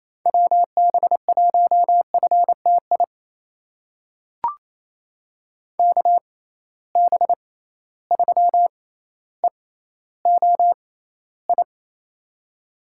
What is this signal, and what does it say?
Morse code, heard as a keyed tone, about 28 wpm, spaced as for 14 wpm, 700 Hz. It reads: WB1FTI KB3EOI
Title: USA Call Signs - 6 Characters - 6x Character Spacing - 28wpm